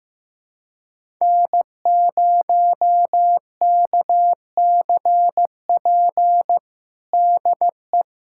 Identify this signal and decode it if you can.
Morse code, heard as a keyed tone, about 15 wpm, 700 Hz. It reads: N0KCP DE
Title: USA Callsign DE Callsign 15wpm